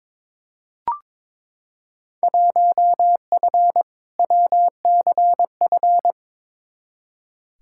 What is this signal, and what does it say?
Morse code, heard as a keyed tone, about 22 wpm, 700 Hz. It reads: 1F WCF
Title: Field Day Exchanges - 22wpm